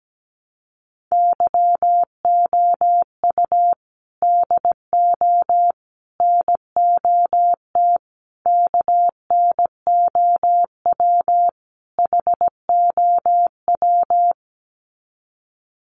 Morse code, heard as a keyed tone, about 17 wpm, 700 Hz. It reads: YOU DO NOT KNOW HOW
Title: Sets of 5 Words Encore 17wpm